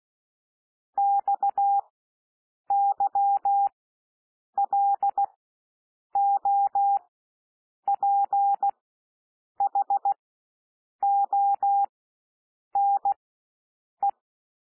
Morse code, heard as a keyed tone, about 16 wpm, spaced as for 7 wpm, 800 Hz. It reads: XYLOPHONE
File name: SOUND WORDS 16 wpm